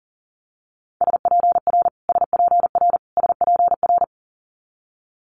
Morse code, heard as a keyed tone, about 38 wpm, 700 Hz. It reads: SPR SPR SPR
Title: Multi-letter Phonemes Repeated 3x 40wpm